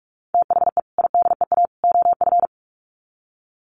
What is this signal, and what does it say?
Morse code, heard as a keyed tone, about 44 wpm, 700 Hz. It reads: THE IDEA OF
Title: Sets of 3 Words 45wpm